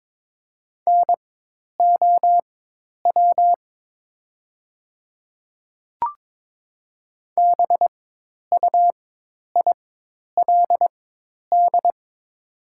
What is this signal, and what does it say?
Morse code, heard as a keyed tone, about 22 wpm, spaced as for 10 wpm, 700 Hz. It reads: NOW BUILD
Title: 4x Character Spacing - Top 200 Words - 22wpm